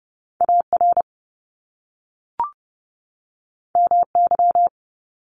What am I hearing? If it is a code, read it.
Morse code, heard as a keyed tone, about 30 wpm, 700 Hz. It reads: AR MY
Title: Top Two Letter Combinations in Words 30wpm